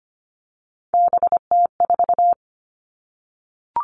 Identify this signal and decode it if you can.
Morse code, heard as a keyed tone, about 25 wpm, 700 Hz. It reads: BT4
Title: All Cut Numbers - 3-Digit 25wpm